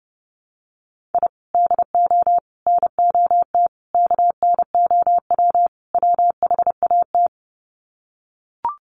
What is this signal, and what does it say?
Morse code, heard as a keyed tone, about 30 wpm, 700 Hz. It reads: I DO NOT KNOW WHAT
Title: Sets of 5 Words 30wpm